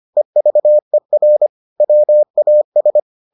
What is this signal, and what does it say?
Morse code, heard as a keyed tone, about 25 wpm, 600 Hz. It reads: EVER WAS